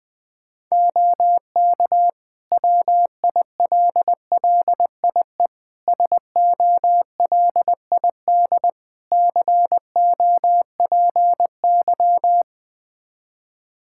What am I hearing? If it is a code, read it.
Morse code, heard as a keyed tone, about 20 wpm, 700 Hz. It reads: OK WILLIE SOLID COPY
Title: Practice QSO - Exchange 3 - More Info - 20wpm